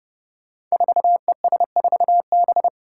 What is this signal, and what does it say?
Morse code, heard as a keyed tone, about 30 wpm, 700 Hz. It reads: 4ES4B
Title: Five Challenging Letter-Number Combinations 30wpm